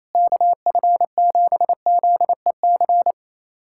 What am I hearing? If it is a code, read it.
Morse code, heard as a keyed tone, about 28 wpm, 700 Hz. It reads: KF7ZEC
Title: USA Callsign DE Callsign - 6 Characters - 28wpm